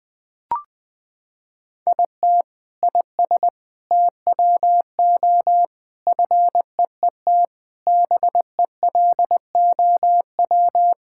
Morse code, heard as a keyed tone, about 20 wpm, 700 Hz. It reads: IT IS TWO FEET BELOW
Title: Sentences from Top 1000 Words - Review Part 5 - 20wpm